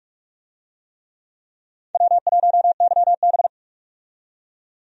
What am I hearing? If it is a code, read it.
Morse code, heard as a keyed tone, about 44 wpm, 700 Hz. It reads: W1YB
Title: USA Call Signs - 4 Characters - 45wpm